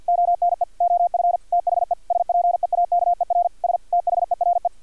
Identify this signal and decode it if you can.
Morse code, heard as a keyed tone, about 50 wpm, 700 Hz, with radio noise. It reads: ONE OF THE SPEAKERS THERE
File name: archived / CRSnet